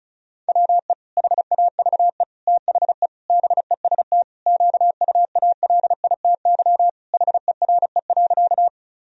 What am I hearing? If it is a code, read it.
Morse code, heard as a keyed tone, about 36 wpm, 700 Hz. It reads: WE HAVE THE BEST QUALITY HERE.
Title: Sentences from Top 1000 Words - Review Part 4 - 35wpm